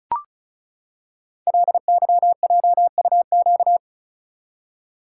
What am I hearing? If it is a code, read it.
Morse code, heard as a keyed tone, about 35 wpm, 700 Hz. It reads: LYJUQ
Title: Code Groups - 5 Letters - 35wpm